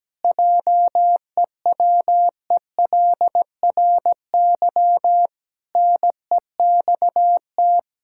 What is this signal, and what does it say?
Morse code, heard as a keyed tone, about 17 wpm, 700 Hz. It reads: JEWELRY NEXT
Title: Pangram Challenge v2 - 17wpm